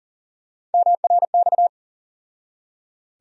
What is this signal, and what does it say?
Morse code, heard as a keyed tone, about 38 wpm, 700 Hz. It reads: MRX
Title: USA Call Sign Postfixes - 3 Letters 40wpm